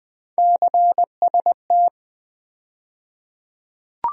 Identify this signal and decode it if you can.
Morse code, heard as a keyed tone, about 21 wpm, 700 Hz. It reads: CST
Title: USA Call Sign Postfixes - 3 Letters 20wpm